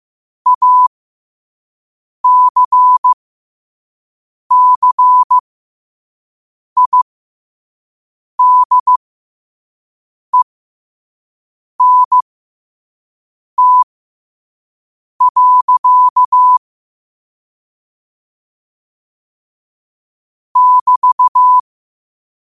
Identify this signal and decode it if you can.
Morse code, heard as a keyed tone, about 15 wpm, spaced as for 5 wpm, 1000 Hz. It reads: ACCIDENT. <BT>
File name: qotdfiles / 5wpm